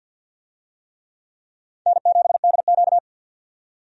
Morse code, heard as a keyed tone, about 49 wpm, 700 Hz. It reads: N7DQ
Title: USA Call Signs - 4 Characters - 50wpm